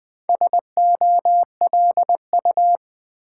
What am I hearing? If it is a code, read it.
Morse code, heard as a keyed tone, about 20 wpm, 700 Hz. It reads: SOLU
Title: Build A Word - 2 Letters At A Time - 8-Letter Words - 20wpm